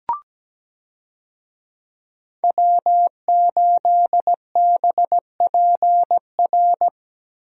Morse code, heard as a keyed tone, about 17 wpm, 700 Hz. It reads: W8BPR
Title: USA Call Signs - 5 Characters - Repeated 3x - 17wpm